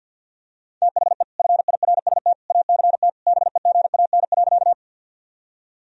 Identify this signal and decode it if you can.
Morse code, heard as a keyed tone, about 53 wpm, 700 Hz. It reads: THE FIRST ACT BEGAN.
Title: Sentences from Top 1000 Words - Review Part 1 - 50wpm